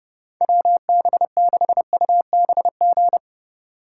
Morse code, heard as a keyed tone, about 32 wpm, 700 Hz. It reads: WB6UBG
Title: USA Call Signs - 6 Characters - 4x Character Spacing - 30wpm